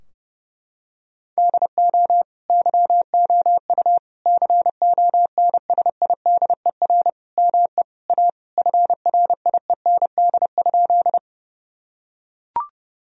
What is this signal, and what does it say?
Morse code, heard as a keyed tone, about 32 wpm, 700 Hz. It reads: DO YOU CONSIDER ME A FRIEND?
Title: Sentences from Top 1000 Words - Review Part 4 - 30wpm